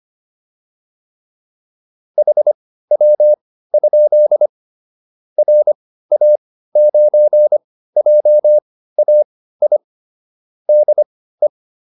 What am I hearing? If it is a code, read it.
Morse code, heard as a keyed tone, about 25 wpm, spaced as for 15 wpm, 600 Hz. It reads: HW? RA9JAI DE
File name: QSO401 15